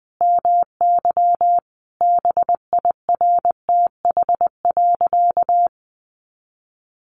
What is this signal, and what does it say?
Morse code, heard as a keyed tone, about 20 wpm, 700 Hz. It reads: MY BIRTH.
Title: Sentences from Top 800 Words 20wpm